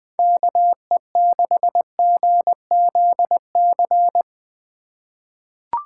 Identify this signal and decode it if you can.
Morse code, heard as a keyed tone, about 20 wpm, 700 Hz. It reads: KE6GZC
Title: USA Call Signs - 6 Characters - 3x Character Spacing - 20wpm